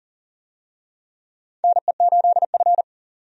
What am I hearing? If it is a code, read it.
Morse code, heard as a keyed tone, about 40 wpm, 700 Hz. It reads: NE8F
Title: USA Call Signs - 4 Characters - 40wpm